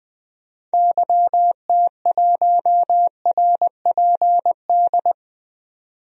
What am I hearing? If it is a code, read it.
Morse code, heard as a keyed tone, about 20 wpm, 700 Hz. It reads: YT1RPD
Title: DXCC-100 Call Signs - 6 Characters - 20wpm